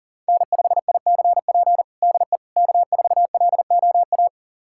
Morse code, heard as a keyed tone, about 40 wpm, 700 Hz. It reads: N5ICP DE K4LOA
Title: USA Callsign DE Callsign - 5 Characters - 40wpm